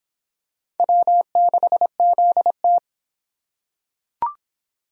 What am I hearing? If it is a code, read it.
Morse code, heard as a keyed tone, about 22 wpm, 700 Hz. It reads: W6ZT